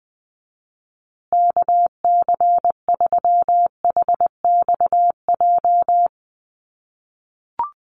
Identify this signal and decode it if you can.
Morse code, heard as a keyed tone, about 20 wpm, 700 Hz. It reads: KC3HXJ
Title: USA Call Signs Repeated 3x 20wpm